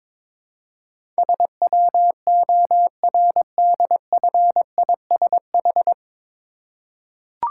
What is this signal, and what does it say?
Morse code, heard as a keyed tone, about 22 wpm, 700 Hz. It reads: SWORDFISH
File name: Difficult Words with Hints - 22wpm